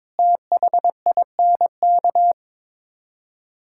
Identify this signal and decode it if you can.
Morse code, heard as a keyed tone, about 22 wpm, 700 Hz. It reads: THINK
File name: practice / 22wpm / Build A Phrase - 5-Word Phrases - 22wpm